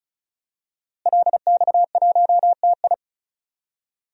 Morse code, heard as a keyed tone, about 36 wpm, 700 Hz. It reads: LX1TI
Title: DXCC-100 Call Signs - 5 Characters - Repeated 3x - 35wpm